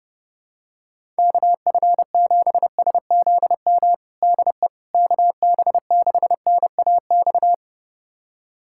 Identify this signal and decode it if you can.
Morse code, heard as a keyed tone, about 32 wpm, 700 Hz. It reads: KF7SZM DE KB6NAX
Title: USA Callsign DE Callsign - 6 Characters - 30wpm